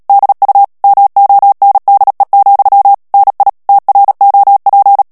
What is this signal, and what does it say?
Morse code, heard as a keyed tone, about 37 wpm, 800 Hz. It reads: DU MONDE, NI TROP
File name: CW / FICHIERS-MP3 / FRANCAIS TEXTE CLAIR 2 VITESSE 35 MOTS MINUTES